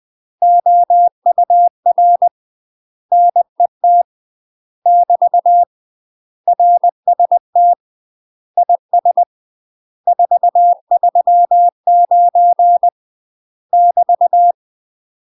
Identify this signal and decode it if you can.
Morse code, heard as a keyed tone, about 20 wpm, 700 Hz. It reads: OUR NET <BT> RST IS 439 <BT>
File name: UK MASTER QSOs TRAINING / 20WPM / UK MASTER QSOs 0017